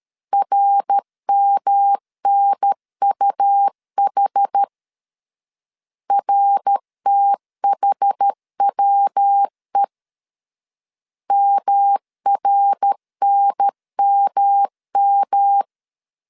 Morse code, heard as a keyed tone, about 13 wpm, 800 Hz. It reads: RMNUH RTHWE MRNMM
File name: CW / DET / 1 3 1500C S C 12wpm 2750